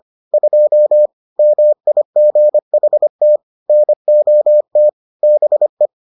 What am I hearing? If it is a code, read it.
Morse code, heard as a keyed tone, about 25 wpm, 600 Hz. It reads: 2 MIGHT NOT BE